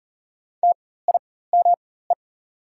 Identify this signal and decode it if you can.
Morse code, heard as a keyed tone, about 37 wpm, spaced as for 18 wpm, 700 Hz. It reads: TIME
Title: 4x Character Spacing - Top 100 Words - 40wpm